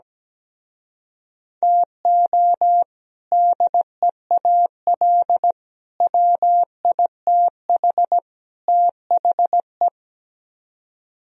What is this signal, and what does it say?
Morse code, heard as a keyed tone, about 17 wpm, 700 Hz. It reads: TO DEAL WITH THE
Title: Speed-Racing - Sets of 4 Words 17wpm